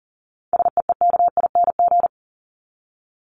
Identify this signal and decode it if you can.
Morse code, heard as a keyed tone, about 40 wpm, 700 Hz. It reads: SEEKING
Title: ING Suffix - Top 100 Words with Base Words 40wpm